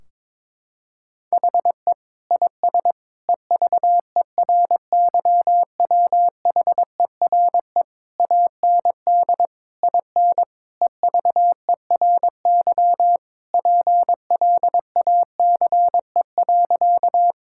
Morse code, heard as a keyed tone, about 22 wpm, 700 Hz. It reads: HE IS EVERYWHERE AND IN EVERY PLACE.
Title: Sentences from Top 1600 Words 22wpm